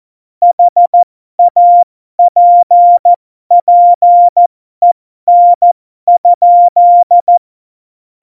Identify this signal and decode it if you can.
Morse code, heard as a keyed tone, about 14 wpm, spaced as for 12 wpm, 700 Hz. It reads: HAPPEN?
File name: Morsekurs / Witze